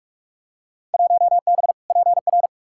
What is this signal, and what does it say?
Morse code, heard as a keyed tone, about 46 wpm, 700 Hz. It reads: 1B PR
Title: Field Day Exchanges Encore - 45wpm